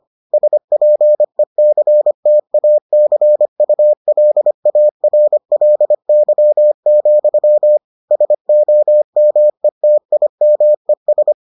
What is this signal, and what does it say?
Morse code, heard as a keyed tone, about 25 wpm, 600 Hz. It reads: SPECTACULARLY, SOMETIMES